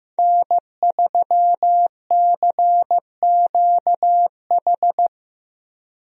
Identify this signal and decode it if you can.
Morse code, heard as a keyed tone, about 15 wpm, 700 Hz. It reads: N3CQH
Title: USA Call Signs - 5 Characters - 15wpm